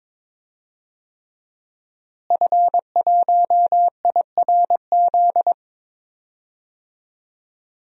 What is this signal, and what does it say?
Morse code, heard as a keyed tone, about 22 wpm, 700 Hz. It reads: F1IRZ
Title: DXCC-100 Call Signs - 5 Characters - Repeated 3x - 22wpm